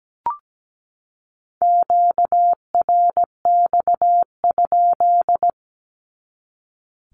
Morse code, heard as a keyed tone, about 17 wpm, 700 Hz. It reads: QRX?
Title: Q-Codes 17wpm